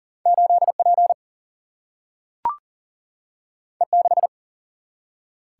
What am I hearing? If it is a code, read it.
Morse code, heard as a keyed tone, about 40 wpm, 700 Hz. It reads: 8 P E 6
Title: Single-Letter Single-Number 40wpm